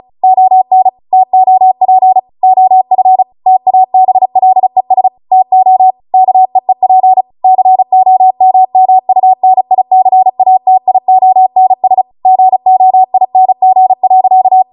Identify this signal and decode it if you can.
Morse code, heard as a keyed tone, about 36 wpm, 750 Hz. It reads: ON TOP OF TABLES TO KEEP COMMUNICATIONS GOING.
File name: Morse / 35WPM